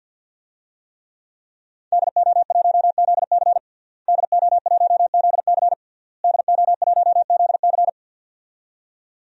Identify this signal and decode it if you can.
Morse code, heard as a keyed tone, about 51 wpm, 700 Hz. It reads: DO1ZC DO1ZC DO1ZC
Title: DXCC-100 Call Signs - 5 Characters - Repeated 3x - 50wpm